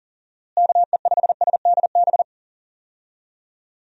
Morse code, heard as a keyed tone, about 37 wpm, 700 Hz. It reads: KE5SDB
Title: USA Call Signs - 6 Characters - Repeated 2x - 40wpm